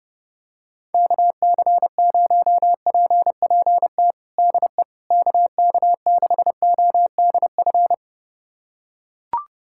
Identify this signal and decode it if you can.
Morse code, heard as a keyed tone, about 32 wpm, 700 Hz. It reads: KC0PPT DE KK6ODF
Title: USA Callsign DE Callsign 30wpm